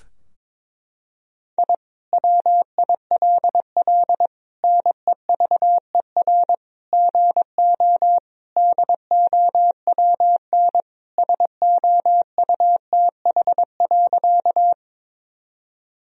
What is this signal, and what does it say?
Morse code, heard as a keyed tone, about 22 wpm, 700 Hz. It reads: I WILL NEVER GO DOWN SOUTH.
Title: Sentences from Top 1000 Words - Review Part 2 - 22wpm